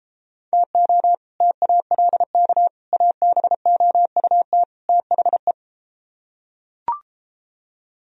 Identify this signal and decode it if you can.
Morse code, heard as a keyed tone, about 32 wpm, 700 Hz. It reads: TO TALK ABOUT THE